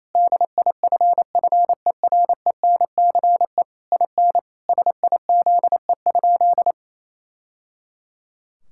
Morse code, heard as a keyed tone, about 28 wpm, 700 Hz. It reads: DIFFERENCE IN SIZE?
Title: Sentences from Top 1000 Words - Review Part 5 - 28wpm